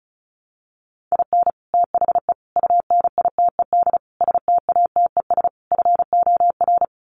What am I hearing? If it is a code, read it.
Morse code, heard as a keyed tone, about 34 wpm, 700 Hz. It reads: IN THE UNITED STATES FOR
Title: Sets of 5 Words Encore2 35wpm